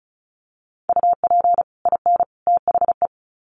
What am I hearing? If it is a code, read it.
Morse code, heard as a keyed tone, about 34 wpm, 700 Hz. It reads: UP IN THE
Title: Sets of 3 Words 35wpm